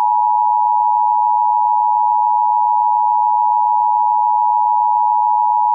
Emergency Alert System (SAME) tones (the 853 and 960 Hz attention tone). It is heard from 0.5 to 4.7 s.